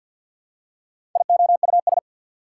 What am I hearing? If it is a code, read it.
Morse code, heard as a keyed tone, about 52 wpm, 690 Hz. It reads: IOUS
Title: Prefixes and Suffixes 50wpm